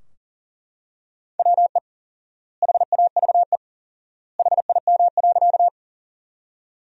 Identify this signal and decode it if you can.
Morse code, heard as a keyed tone, about 40 wpm, 700 Hz. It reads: WE HAVE HIM.
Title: 4x Word-Spacing - Sentences from Top 100 Words - 40wpm